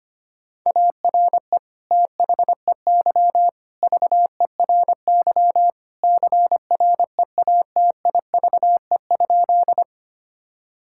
Morse code, heard as a keyed tone, about 25 wpm, 700 Hz. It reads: ARE THEY VERY CREATIVE?
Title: Sentences from Top 1000 Words - Review Part 5 - 25wpm